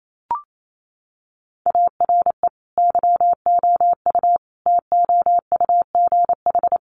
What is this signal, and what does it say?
Morse code, heard as a keyed tone, about 28 wpm, 700 Hz. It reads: ARE YOU TOUGH
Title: Sentences from Top 1200 Words 28wpm